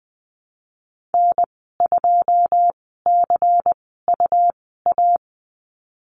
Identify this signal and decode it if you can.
Morse code, heard as a keyed tone, about 20 wpm, spaced as for 15 wpm, 700 Hz. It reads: N2CUA
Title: 173 Lesson 28g - Number 9 - Call Signs 20wpm